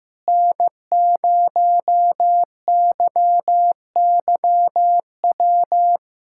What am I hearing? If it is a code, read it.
Morse code, heard as a keyed tone, about 15 wpm, 700 Hz. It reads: N0YYW
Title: USA Callsign DE Callsign - 5 Characters - 15wpm